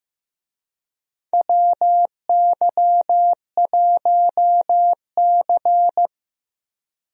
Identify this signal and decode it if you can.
Morse code, heard as a keyed tone, about 15 wpm, 700 Hz. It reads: WY1C
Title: USA Call Signs - 4 Characters - Repeated 3x - 15wpm